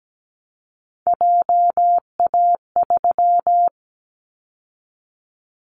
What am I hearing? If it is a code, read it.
Morse code, heard as a keyed tone, about 17 wpm, 700 Hz. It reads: JA3
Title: Callsign Prefixes for Top 50 DXCC - 17wpm